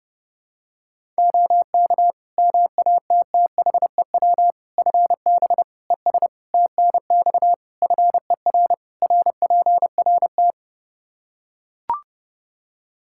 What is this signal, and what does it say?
Morse code, heard as a keyed tone, about 30 wpm, 700 Hz. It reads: OK MATTHEW FB ES TNX FER RPRT
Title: Practice QSO - Exchange 2 - Info - 30wpm